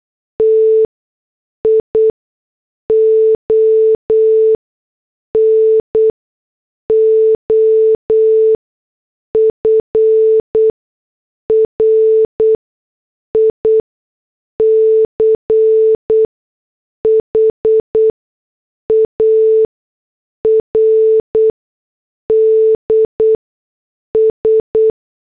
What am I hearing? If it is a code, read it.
Morse code, heard as a keyed tone, about 8 wpm, spaced as for 6 wpm, 440 Hz. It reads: TION OF RICHARDS
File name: morse / headlines